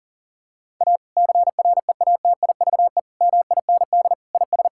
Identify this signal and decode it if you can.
Morse code, heard as a keyed tone, about 40 wpm, 700 Hz. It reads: A CREATIVE MIND IS
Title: Sentences from Top 1000 Words - Review Part 5 - 40wpm